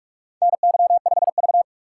Morse code, heard as a keyed tone, about 46 wpm, 700 Hz. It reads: NY5V
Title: USA Call Signs - 4 Characters - 2x Character Spacing - 45wpm